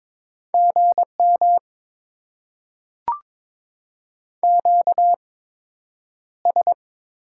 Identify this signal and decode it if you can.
Morse code, heard as a keyed tone, about 22 wpm, spaced as for 9 wpm, 700 Hz. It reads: GM QS
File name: Top 90 QSO Elements - 8x Character Spacing - 22wpm